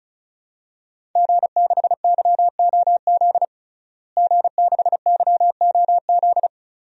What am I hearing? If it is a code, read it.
Morse code, heard as a keyed tone, about 34 wpm, 700 Hz. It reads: G6YOZ G6YOZ
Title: DXCC-100 Call Signs - 5 Characters - Repeated 2x - 35wpm